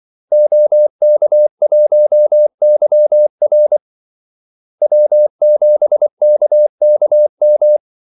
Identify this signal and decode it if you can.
Morse code, heard as a keyed tone, about 24 wpm, 600 Hz. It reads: OK1YR W7KKM
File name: Callsigns-24wpm